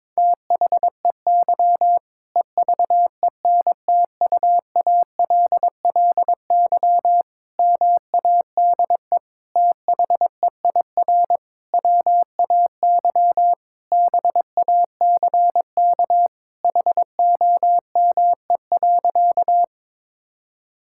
Morse code, heard as a keyed tone, about 21 wpm, 700 Hz. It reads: THEY EVENTUALLY MADE THEIR WAY BACK HOME.
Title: Sentences from Top 1000 Words - Review Part 6 - 22wpm